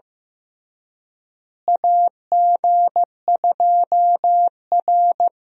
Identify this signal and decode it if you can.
Morse code, heard as a keyed tone, about 15 wpm, 700 Hz. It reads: AG2R
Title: USA Call Signs - 4 Characters - Repeated 3x - 15wpm